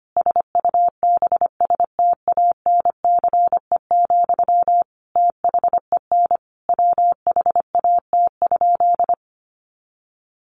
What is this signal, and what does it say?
Morse code, heard as a keyed tone, about 25 wpm, 700 Hz. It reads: SUBSTANCE, THEN WHAT?
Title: Sentences from Top 1500 Words 25wpm